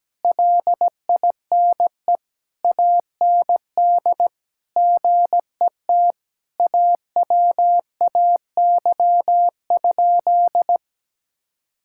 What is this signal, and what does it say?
Morse code, heard as a keyed tone, about 17 wpm, 700 Hz. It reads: LINE AND GET AWAY?
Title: Sentences from Top 1000 Words - Review Part 6 - 17wpm